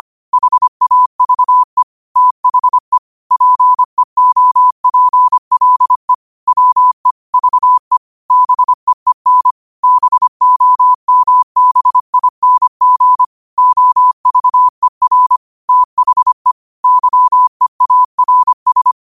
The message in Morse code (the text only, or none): HAVE THE PEOPLE WE VE BEEN BOMBING OVER THE YEARS